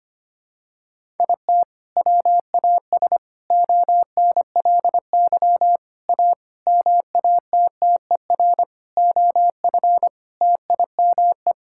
Morse code, heard as a keyed tone, about 25 wpm, 700 Hz. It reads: IT WAS ONLY A MATTER OF TIME
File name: Sets of 9 Words - 25wpm